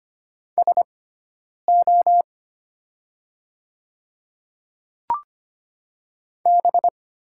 Morse code, heard as a keyed tone, about 25 wpm, spaced as for 9 wpm, 700 Hz. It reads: SO B
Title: 6x Character Spacing - Top 300 Words - 25wpm